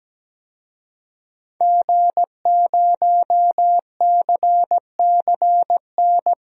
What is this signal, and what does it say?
Morse code, heard as a keyed tone, about 17 wpm, 700 Hz. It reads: G0CCN
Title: DXCC-100 Call Signs - 5 Characters - Repeated 3x - 17wpm